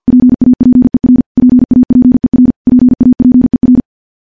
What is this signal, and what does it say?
Morse code, heard as a keyed tone, about 50 wpm, 250 Hz. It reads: PAPER PAPER PAPER
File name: morse-training-50WPM